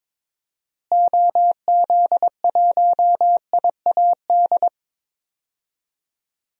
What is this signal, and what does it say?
Morse code, heard as a keyed tone, about 22 wpm, 700 Hz. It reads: OZ1IAD
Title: DXCC-100 Call Signs - 6 Characters - Repeated 2x - 22wpm